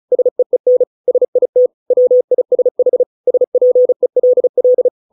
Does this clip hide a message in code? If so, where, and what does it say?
Yes: Morse code from 0.1 to 4.9 s, heard as a keyed tone, about 34 wpm, 500 Hz. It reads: SEEN SIT WISH SPELL